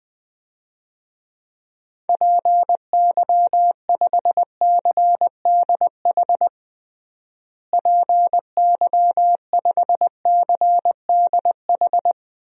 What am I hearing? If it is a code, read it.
Morse code, heard as a keyed tone, about 20 wpm, 700 Hz. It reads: PY5CDH PY5CDH
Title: DXCC-100 Call Signs - 6 Characters - Repeated 3x - 20wpm